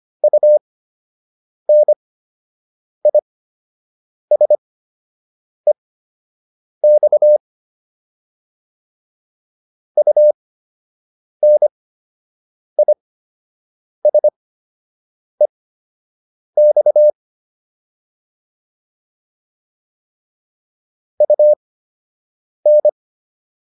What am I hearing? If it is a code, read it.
Morse code, heard as a keyed tone, about 25 wpm, spaced as for 7 wpm, 600 Hz. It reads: UNISEX UNISEX UN